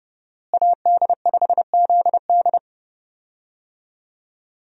Morse code, heard as a keyed tone, about 32 wpm, 700 Hz. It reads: AD5ZD
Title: USA Call Signs - 5 Characters - Repeated 3x - 30wpm